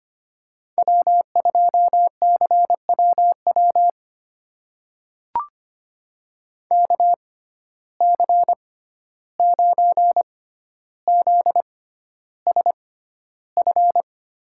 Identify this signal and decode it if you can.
Morse code, heard as a keyed tone, about 25 wpm, spaced as for 12 wpm, 700 Hz. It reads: W2CWW KC9ZSF
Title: USA Call Signs - 6x Character Spacing - 25wpm